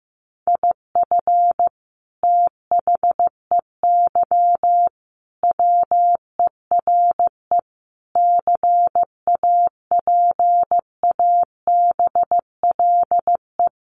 Morse code, heard as a keyed tone, about 15 wpm, 700 Hz. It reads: IF THEY WERE CAPABLE
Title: Sentences from Top 1200 Words 15wpm